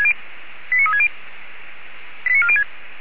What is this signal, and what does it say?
Five-tone selective call (ZVEI), heard as quick runs of five tones, tones of about 70 ms. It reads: 89360 89406
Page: audio.f85360.feuerwehr-orlamuende.org